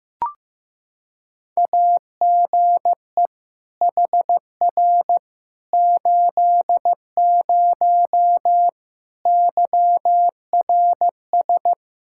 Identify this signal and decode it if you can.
Morse code, heard as a keyed tone, about 15 wpm, 700 Hz. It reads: AGE HR 80 YRS
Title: K1USN SST QSO Practice 15wpm